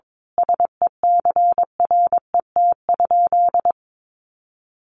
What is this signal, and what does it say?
Morse code, heard as a keyed tone, about 22 wpm, 700 Hz. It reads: SECRET?
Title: Sentences from Top 1200 Words 22wpm